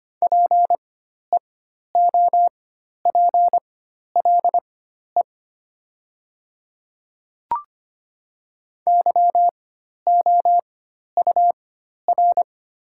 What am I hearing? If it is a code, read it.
Morse code, heard as a keyed tone, about 25 wpm, spaced as for 12 wpm, 700 Hz. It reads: PEOPLE YOUR
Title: 4x Character Spacing - Top 100 Words - 25wpm